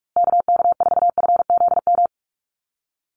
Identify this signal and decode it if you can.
Morse code, heard as a keyed tone, about 44 wpm, 700 Hz. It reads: KK4FZM
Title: USA Call Signs Repeated 2x 45wpm